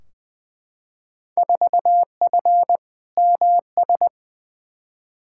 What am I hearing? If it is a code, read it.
Morse code, heard as a keyed tone, about 20 wpm, 700 Hz. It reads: 4F MS
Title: Field Day Exchanges Encore - 20wpm